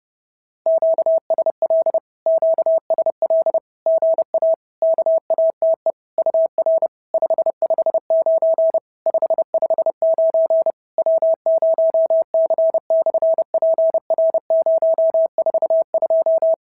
Morse code, heard as a keyed tone, about 30 wpm, 650 Hz. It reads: QSL QSL GA KATE UR 559 559 W0C/PR042
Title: SOTA QSO Practice 30wpm